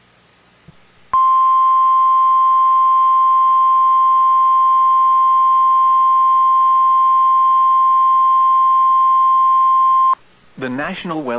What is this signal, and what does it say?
Emergency Alert System (SAME) tones (the 1050 Hz attention tone), with speech. It is heard from 1.1 to 10.2 s.